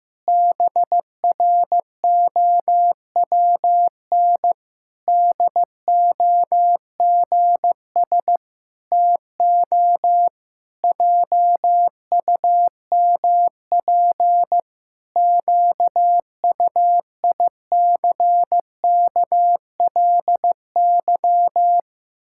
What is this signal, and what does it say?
Morse code, heard as a keyed tone, about 15 wpm, 700 Hz. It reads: BROWN DOGS TO JUMP QUICKLY
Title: Pangram Challenge v2 - 15wpm